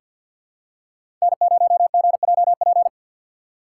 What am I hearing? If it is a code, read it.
Morse code, heard as a keyed tone, about 50 wpm, 700 Hz. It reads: N0GJP
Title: USA Call Signs - 5 Characters - 4x Character Spacing - 50wpm